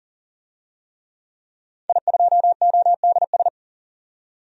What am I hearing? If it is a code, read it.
Morse code, heard as a keyed tone, about 40 wpm, 700 Hz. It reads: I2ODS
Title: DXCC-100 Call Signs - 5 Characters - 40wpm